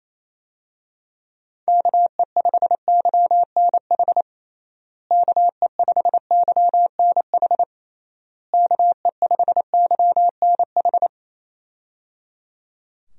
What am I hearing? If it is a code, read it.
Morse code, heard as a keyed tone, about 28 wpm, 700 Hz. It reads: KE5YNH KE5YNH KE5YNH
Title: USA Call Signs - 6 Characters - Repeated 3x - 28wpm